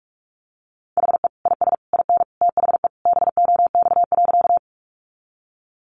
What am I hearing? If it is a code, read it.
Morse code, heard as a keyed tone, about 44 wpm, 700 Hz. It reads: HE IS IN THE BOX.
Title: Sentences from Top 500 Words 45wpm